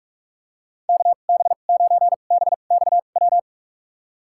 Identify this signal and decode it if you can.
Morse code, heard as a keyed tone, about 46 wpm, spaced as for 32 wpm, 700 Hz. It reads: KB9BXW
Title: USA Call Signs - 6 Characters - 2x Character Spacing - 45wpm